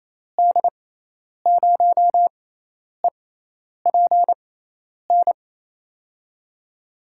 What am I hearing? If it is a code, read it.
Morse code, heard as a keyed tone, about 28 wpm, spaced as for 10 wpm, 700 Hz. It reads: D0EPN
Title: USA Call Signs - 6 Characters - 6x Character Spacing - 28wpm